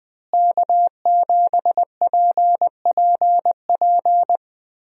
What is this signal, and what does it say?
Morse code, heard as a keyed tone, about 21 wpm, 700 Hz. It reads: K7PPP
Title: USA Call Signs - 5 Characters - Repeated 2x - 20wpm